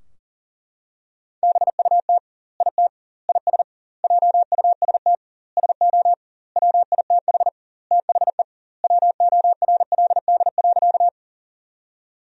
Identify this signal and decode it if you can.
Morse code, heard as a keyed tone, about 40 wpm, 700 Hz. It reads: BUT IT IS JUST SO WITH THE WORLD.
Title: 2x Word-Spacing - Sentences from Top 200 Words - 40wpm